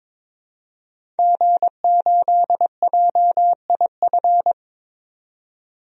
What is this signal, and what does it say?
Morse code, heard as a keyed tone, about 22 wpm, 700 Hz. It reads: G8JIF
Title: DXCC-100 Call Signs - 5 Characters - 22wpm